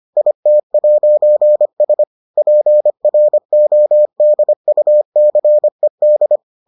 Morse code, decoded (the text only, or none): IT'S PRODUCED